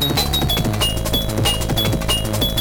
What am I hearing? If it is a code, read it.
There is no code.